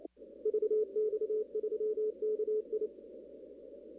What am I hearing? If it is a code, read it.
Morse code, heard as a keyed tone, about 28 wpm, 430 Hz. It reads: VX3KI
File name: CW / kt5j wpx cw-op-k5pi